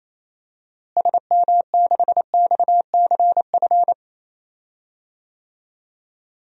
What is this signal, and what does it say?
Morse code, heard as a keyed tone, about 28 wpm, 700 Hz. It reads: SM6XCF
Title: DXCC-100 Call Signs - 6 Characters - Repeated 3x - 28wpm